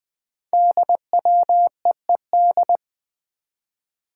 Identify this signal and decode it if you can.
Morse code, heard as a keyed tone, about 20 wpm, 700 Hz. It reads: DWEED
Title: Code Groups - 5 Letter:Numbers - 20wpm